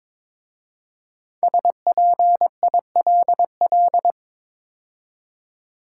Morse code, heard as a keyed tone, about 22 wpm, 700 Hz. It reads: SPILL
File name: 5 Letter Words - First 3 Letters Given - 22wpm